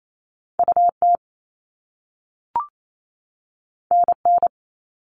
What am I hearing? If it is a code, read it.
Morse code, heard as a keyed tone, about 28 wpm, 700 Hz. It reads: UT NN
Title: All Cut Numbers - 2-Digit 28wpm